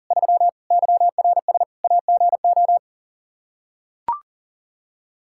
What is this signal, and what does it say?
Morse code, heard as a keyed tone, about 40 wpm, 700 Hz. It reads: 3 YRS AGO
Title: Practice QSO - Exchange 3 - More Info - 40wpm